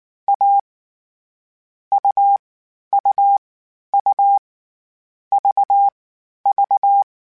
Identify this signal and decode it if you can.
Morse code, heard as a keyed tone, about 19 wpm, 800 Hz. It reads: A U U U V V
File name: CW Code Groups-mp3s / Sets mp3s / AUV AUV4 sp @18